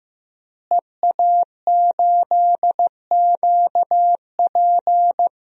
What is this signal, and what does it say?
Morse code, heard as a keyed tone, about 16 wpm, 700 Hz. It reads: EA8QP
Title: DXCC-100 Call Signs - 5 Characters - Repeated 3x - 15wpm